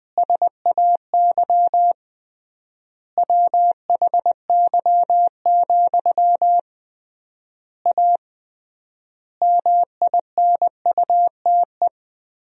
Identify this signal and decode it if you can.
Morse code, heard as a keyed tone, about 20 wpm, 700 Hz. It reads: SAY WHY, A MINUTE
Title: 3x Word-Spacing - Sentences from Top 500 Words - 20wpm